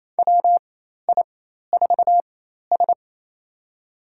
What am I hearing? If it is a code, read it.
Morse code, heard as a keyed tone, about 28 wpm, spaced as for 13 wpm, 700 Hz. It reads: WI4S
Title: USA Call Signs - 4 Characters - 4x Character Spacing - 28wpm